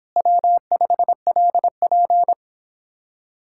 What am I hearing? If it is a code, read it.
Morse code, heard as a keyed tone, about 29 wpm, 700 Hz. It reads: W5LP